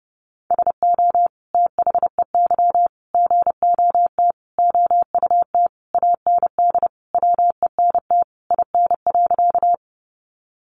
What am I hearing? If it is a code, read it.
Morse code, heard as a keyed tone, about 30 wpm, 700 Hz. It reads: SO THEY GOT OUT AND WENT IN.
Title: Sentences from Top 300 Words 30wpm